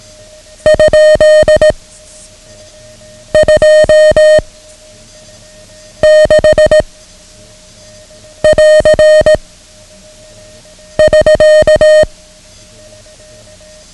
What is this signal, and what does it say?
Morse code, heard as a keyed tone, about 18 wpm, 620 Hz. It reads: ? 2 6 <AR> <SK>
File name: ham radio / CodeQuick / Code Quick 09